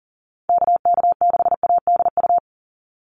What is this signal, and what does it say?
Morse code, heard as a keyed tone, about 40 wpm, 700 Hz. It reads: KK6ADU
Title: USA Call Signs 40wpm